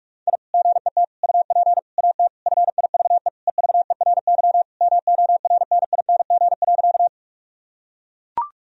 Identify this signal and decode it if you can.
Morse code, heard as a keyed tone, about 42 wpm, 700 Hz. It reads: I GET UP AT FIVE EVERY MORNING.
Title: Sentences from Top 1000 Words - Review Part 2 - 45wpm